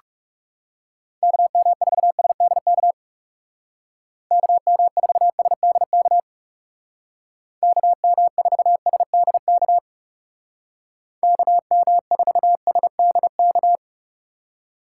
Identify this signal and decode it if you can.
Morse code, heard as a keyed tone, about 37 wpm, 700 Hz. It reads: KM4SDK KM4SDK KM4SDK KM4SDK
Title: Speed Racing - USA 2x3 Call Signs 30wpm